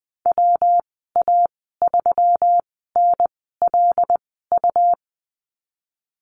Morse code, heard as a keyed tone, about 20 wpm, spaced as for 15 wpm, 700 Hz. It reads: WA3NLU
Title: 161 Lesson 26g - Letter g - Call Signs 20wpm